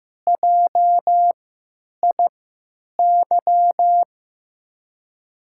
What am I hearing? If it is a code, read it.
Morse code, heard as a keyed tone, about 15 wpm, spaced as for 9 wpm, 700 Hz. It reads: JIY
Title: USA Call Signs - 6 Characters - 3x Character Spacing - 15wpm